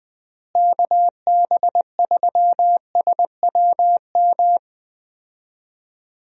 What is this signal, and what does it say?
Morse code, heard as a keyed tone, about 20 wpm, 700 Hz. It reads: KB3SWM
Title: USA Call Signs - 6 Characters - Repeated 3x - 20wpm